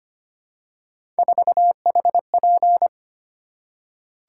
Morse code, heard as a keyed tone, about 25 wpm, 700 Hz. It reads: 4HP
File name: Three Letter-Number Combinations - 25wpm